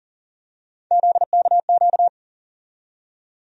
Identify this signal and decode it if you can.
Morse code, heard as a keyed tone, about 40 wpm, 700 Hz. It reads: ZKQ
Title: USA Call Sign Postfixes - 3 Letters 40wpm